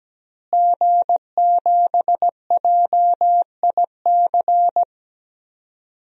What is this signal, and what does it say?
Morse code, heard as a keyed tone, about 17 wpm, 700 Hz. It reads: G7JIC
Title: DXCC-100 Call Signs - 5 Characters - Repeated 2x - 17wpm